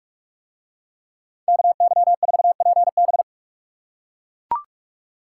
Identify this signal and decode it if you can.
Morse code, heard as a keyed tone, about 44 wpm, 700 Hz. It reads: KY4PB